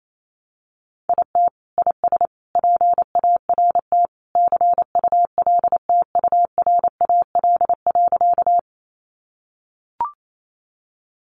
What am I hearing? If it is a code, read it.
Morse code, heard as a keyed tone, about 28 wpm, 700 Hz. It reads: IT IS PART CULTURAL.
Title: Sentences from Top 1500 Words 28wpm